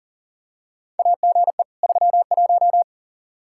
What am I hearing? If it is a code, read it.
Morse code, heard as a keyed tone, about 40 wpm, 700 Hz. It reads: AGE 31
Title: Practice QSO - Exchange 3 - More Info - 40wpm